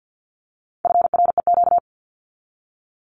Morse code, heard as a keyed tone, about 50 wpm, 700 Hz. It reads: FREQ
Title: Abbreviations 50wpm